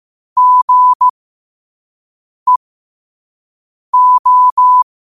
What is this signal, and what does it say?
Morse code, heard as a keyed tone, about 15 wpm, spaced as for 5 wpm, 1000 Hz. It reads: GEO
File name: qotdfiles / 5wpm